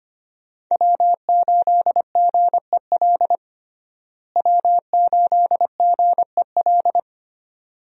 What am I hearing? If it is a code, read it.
Morse code, heard as a keyed tone, about 25 wpm, 700 Hz. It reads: W8GEL W8GEL
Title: USA Call Signs - 5 Characters - Repeated 2x - 25wpm